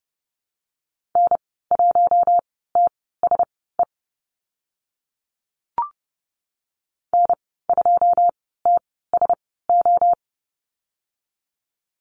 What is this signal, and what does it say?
Morse code, heard as a keyed tone, about 30 wpm, spaced as for 17 wpm, 700 Hz. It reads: N1TSE N2TSO
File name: Lesson 14g - Number 2 - Call Signs 30wpm